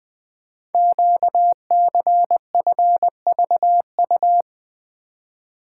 Morse code, heard as a keyed tone, about 20 wpm, 700 Hz. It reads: QCFVU
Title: Code Groups - 5 Letters - 20wpm